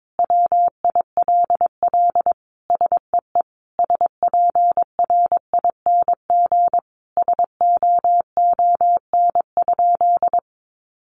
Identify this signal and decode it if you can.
Morse code, heard as a keyed tone, about 22 wpm, 700 Hz. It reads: WILL SEE SPRING SOON?
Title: Sentences from Top 1200 Words 22wpm